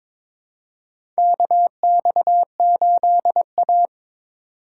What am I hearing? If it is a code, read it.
Morse code, heard as a keyed tone, about 21 wpm, 700 Hz. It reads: KX8A
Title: USA Call Signs - 4 Characters - 2x Character Spacing - 22wpm